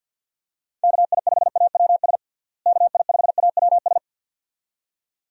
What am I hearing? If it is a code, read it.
Morse code, heard as a keyed tone, about 52 wpm, 700 Hz. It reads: KI5AWS KI5AWS
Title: USA Call Signs - 6 Characters - Repeated 2x - 50wpm